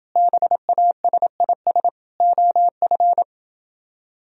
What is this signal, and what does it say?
Morse code, heard as a keyed tone, about 28 wpm, 700 Hz. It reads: BASIS OF